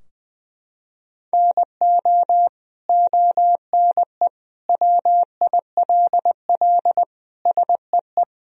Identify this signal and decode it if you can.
Morse code, heard as a keyed tone, about 20 wpm, 700 Hz. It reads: NO ONE WILL SEE
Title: Sentences from Top 1000 Words - Review Part 3 - 20wpm